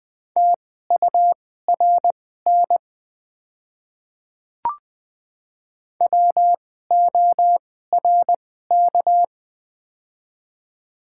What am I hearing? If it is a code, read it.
Morse code, heard as a keyed tone, about 20 wpm, spaced as for 15 wpm, 700 Hz. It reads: TURN WORK
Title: 2x Character Spacing - Top 200 Words - 20wpm